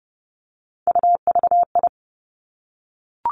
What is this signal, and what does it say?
Morse code, heard as a keyed tone, about 31 wpm, 700 Hz. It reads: UVI
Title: Three Challenging Letter-Number Combinations 30wpm